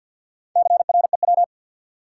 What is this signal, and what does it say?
Morse code, heard as a keyed tone, about 53 wpm, 700 Hz. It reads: CREW
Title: Sentences from Top 1500 Words - New Word Drill - 50wpm